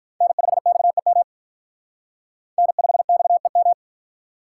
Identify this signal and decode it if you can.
Morse code, heard as a keyed tone, about 45 wpm, 700 Hz. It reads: N5XEM N5XEM